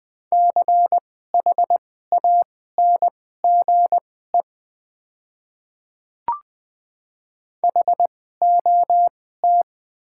Morse code, heard as a keyed tone, about 20 wpm, spaced as for 14 wpm, 700 Hz. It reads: CHANGE HOT
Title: 2x Character Spacing - Top 200 Words - 20wpm